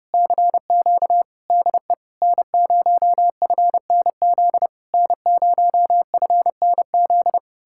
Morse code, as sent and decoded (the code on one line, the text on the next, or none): -.-. --.- / -.. / -. ----- ..-. -. --.. / -. ----- ..-. -. --..
CQ DE N0FNZ N0FNZ